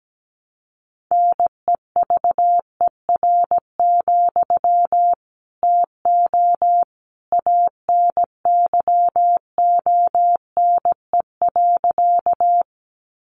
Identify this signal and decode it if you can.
Morse code, heard as a keyed tone, about 17 wpm, 700 Hz. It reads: NEVER, TO ANYONE.
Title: Sentences from Top 900 Words 17wpm